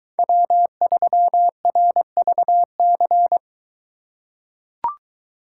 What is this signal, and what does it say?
Morse code, heard as a keyed tone, about 22 wpm, 700 Hz. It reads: W3RVC